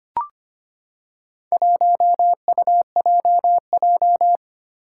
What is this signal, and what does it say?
Morse code, heard as a keyed tone, about 25 wpm, 700 Hz. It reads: 1UJJ
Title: Four Challenging Letter-Number Combinations 25wpm